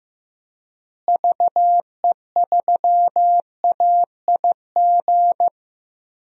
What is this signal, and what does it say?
Morse code, heard as a keyed tone, about 15 wpm, 700 Hz. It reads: VE3AIG
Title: DXCC-100 Call Signs - 6 Characters - Repeated 3x - 15wpm